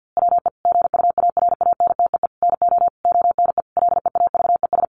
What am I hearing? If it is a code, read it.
Morse code, heard as a keyed tone, about 50 wpm, 700 Hz. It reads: WE GUARANTEE NO ONE LEAVES
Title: A Sound Of Thunder 50wpm